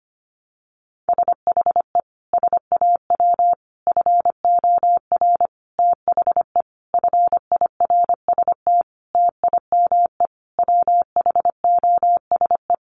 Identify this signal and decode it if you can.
Morse code, heard as a keyed tone, about 25 wpm, 700 Hz. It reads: SHE SAW FOR THE FIRST TIME WHOSE
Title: Sentences from Top 1200 Words 25wpm